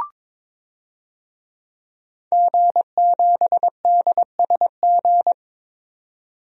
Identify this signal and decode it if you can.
Morse code, heard as a keyed tone, about 22 wpm, 700 Hz. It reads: G7DSG
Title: DXCC-100 Call Signs - 5 Characters - 22wpm